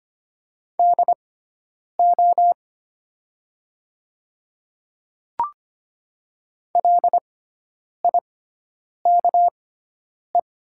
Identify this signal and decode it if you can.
Morse code, heard as a keyed tone, about 25 wpm, spaced as for 9 wpm, 700 Hz. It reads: DO LIKE
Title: 6x Character Spacing - Top 100 Words - 25wpm